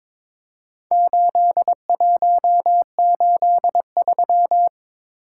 Morse code, heard as a keyed tone, about 23 wpm, 700 Hz. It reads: 8183
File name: practice / 22wpm / Four Letter-Number Combinations - 22wpm